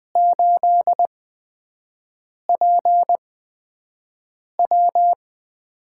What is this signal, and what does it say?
Morse code, heard as a keyed tone, about 20 wpm, spaced as for 5 wpm, 700 Hz. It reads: 8PW
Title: USA Call Signs - 6 Characters - 8x Character Spacing - 20wpm